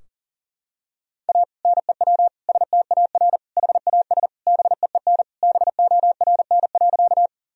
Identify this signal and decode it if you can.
Morse code, heard as a keyed tone, about 40 wpm, 700 Hz. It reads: A NEW STAR HAS BEEN BORN.
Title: Sentences from Top 1000 Words - Review Part 2 - 40wpm